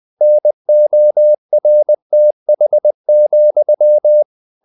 Morse code, heard as a keyed tone, about 20 wpm, 600 Hz. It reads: NORTH,
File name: the old man and the sea / the old man and the sea part 1 20wpm